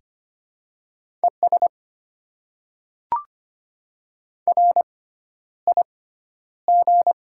Morse code, heard as a keyed tone, about 25 wpm, spaced as for 11 wpm, 700 Hz. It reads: ES RIG
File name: Top 20 QSO Elements - 6x Character Spacing - 25wpm